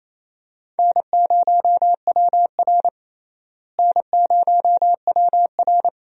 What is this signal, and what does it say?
Morse code, heard as a keyed tone, about 28 wpm, 700 Hz. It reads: N0WR N0WR
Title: USA Call Signs - 4 Characters - Repeated 2x - 28wpm